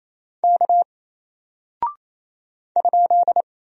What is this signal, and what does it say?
Morse code, heard as a keyed tone, about 28 wpm, 700 Hz. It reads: K ?
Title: Single Character (Letter:Number:Punctuation:Prosign) Alt 28wpm